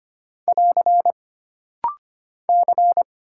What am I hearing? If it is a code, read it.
Morse code, heard as a keyed tone, about 25 wpm, 700 Hz. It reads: <AR> C
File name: Single Character (Letter:Number:Punctuation:Prosign) - Rapid-Fire - 25wpm